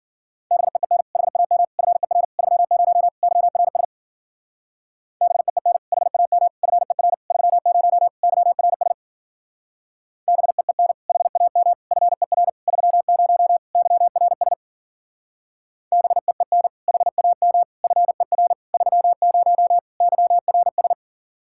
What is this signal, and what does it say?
Morse code, heard as a keyed tone, about 49 wpm, 700 Hz. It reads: BEEN HAM FER 30 YRS BEEN HAM FER 30 YRS BEEN HAM FER 30 YRS BEEN HAM FER 30 YRS
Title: Speed-Racing Common QSO Phrases - 40wpm